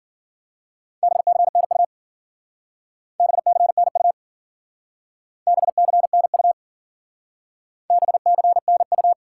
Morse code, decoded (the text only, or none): BCNU BCNU BCNU BCNU